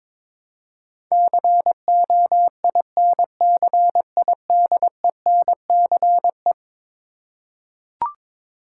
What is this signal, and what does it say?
Morse code, heard as a keyed tone, about 22 wpm, 700 Hz. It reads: COINCIDENCE
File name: practice / 22wpm / Eleven Letter Words - 22wpm